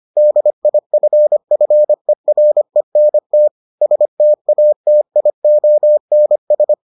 Morse code, heard as a keyed tone, about 25 wpm, 600 Hz. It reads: DIFFERENT STATIONS